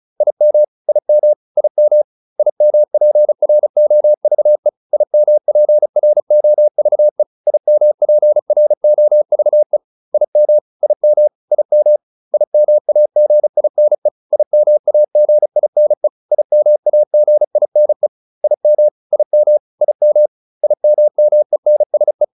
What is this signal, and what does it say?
Morse code, heard as a keyed tone, about 34 wpm, 600 Hz. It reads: IM IM IM IMPROVE IMPROVE IMPROVE IM IM IM IMAGINE IMAGINE IMAGINE IM IM IM IMMENSE